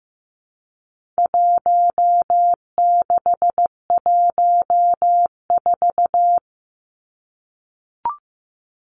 Morse code, heard as a keyed tone, about 15 wpm, 700 Hz. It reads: 1614
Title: Numbers - Contest Style - 4-Digit 15wpm